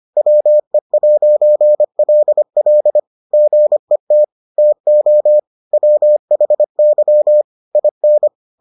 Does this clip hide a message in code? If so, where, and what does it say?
Yes: Morse code from 0.2 to 8.3 s, heard as a keyed tone, about 25 wpm, 600 Hz. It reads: WE'LL GET TO WHY IN